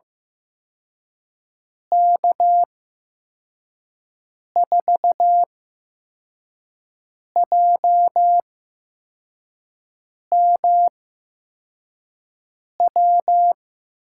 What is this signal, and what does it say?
Morse code, heard as a keyed tone, about 15 wpm, spaced as for 4 wpm, 700 Hz. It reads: K4JMW
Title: USA Call Signs - 6 Characters - 8x Character Spacing - 15wpm